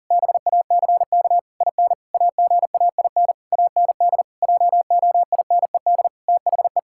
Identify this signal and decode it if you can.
Morse code, heard as a keyed tone, about 37 wpm, 700 Hz. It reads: BACK IN AGAIN AND JOINED THE
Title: Pangram Challenge v2 - 40wpm